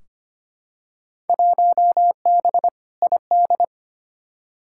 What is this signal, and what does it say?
Morse code, heard as a keyed tone, about 25 wpm, 700 Hz. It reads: 1B ID
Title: Field Day Exchanges - 25wpm